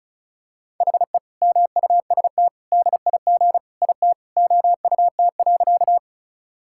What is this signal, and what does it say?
Morse code, heard as a keyed tone, about 36 wpm, 700 Hz. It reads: HE MUST DIG IT OUT.
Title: Sentences from Top 1600 Words 35wpm